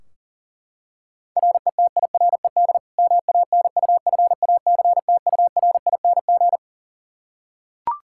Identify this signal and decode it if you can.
Morse code, heard as a keyed tone, about 40 wpm, 700 Hz. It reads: RETIRED MANUFACTURING
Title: K1USN SST QSO Practice 40wpm